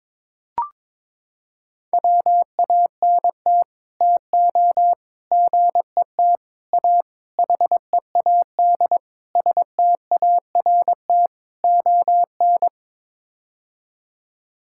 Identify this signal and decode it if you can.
Morse code, heard as a keyed tone, about 23 wpm, 700 Hz. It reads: WANT TO GET A HEAD START ON
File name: practice / 22wpm / Sets of 7 Words - 22wpm